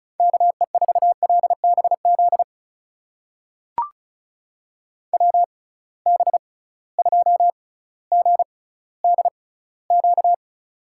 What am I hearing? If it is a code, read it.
Morse code, heard as a keyed tone, about 36 wpm, spaced as for 18 wpm, 700 Hz. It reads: KE4LBZ WB2GDQ
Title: USA Call Signs - 6 Characters - 6x Character Spacing - 35wpm